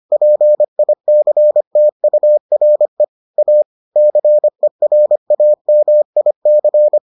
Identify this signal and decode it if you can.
Morse code, heard as a keyed tone, about 25 wpm, 600 Hz. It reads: PICTURE A CERAMIC